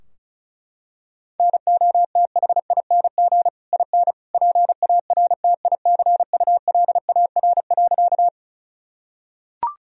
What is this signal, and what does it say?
Morse code, heard as a keyed tone, about 34 wpm, 700 Hz. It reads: NOTHING IN PARTICULAR.
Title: Sentences from Top 1000 Words - Review Part 4 - 35wpm